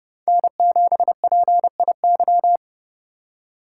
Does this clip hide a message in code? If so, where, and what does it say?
Yes: Morse code from 0.3 to 2.6 s, heard as a keyed tone, about 38 wpm, 700 Hz. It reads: N7PIY